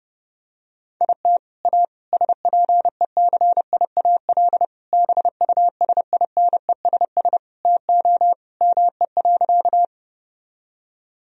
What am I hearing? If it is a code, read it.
Morse code, heard as a keyed tone, about 30 wpm, 700 Hz. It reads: IT A SPECIAL BUSINESS TO ME.
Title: Sentences from Top 1000 Words - Review Part 3 - 30wpm